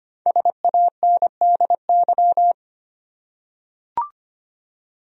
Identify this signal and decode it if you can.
Morse code, heard as a keyed tone, about 25 wpm, 700 Hz. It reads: SANDY
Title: USA Common Names - Max 5 Letters - 25wpm